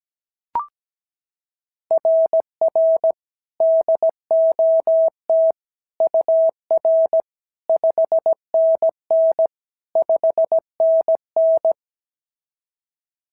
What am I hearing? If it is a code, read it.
Morse code, heard as a keyed tone, about 16 wpm, 650 Hz. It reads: RR DOT UR 5NN 5NN
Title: SOTA QSO Practice 17wpm